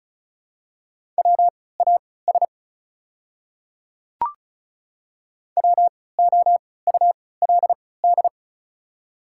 Morse code, heard as a keyed tone, about 36 wpm, spaced as for 20 wpm, 690 Hz. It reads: WAS WOULD
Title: 3x Character Spacing - Top 100 Words - 35wpm